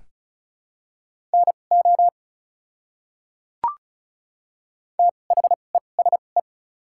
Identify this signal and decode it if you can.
Morse code, heard as a keyed tone, about 34 wpm, spaced as for 25 wpm, 690 Hz. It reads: NO THESE
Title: 2x Character Spacing - Top 100 Words - 35wpm